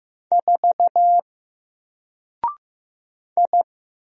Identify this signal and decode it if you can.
Morse code, heard as a keyed tone, about 15 wpm, 700 Hz. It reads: 4 I
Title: Single Letter-Number 15wpm